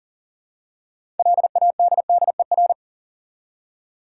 Morse code, heard as a keyed tone, about 40 wpm, 700 Hz. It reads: LADDER
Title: Six Letter Words - 40wpm